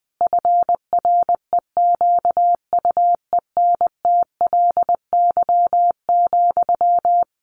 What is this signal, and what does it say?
Morse code, heard as a keyed tone, about 20 wpm, 700 Hz. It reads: FREQUENTLY,
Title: Sentences from Top 1300 Words 20wpm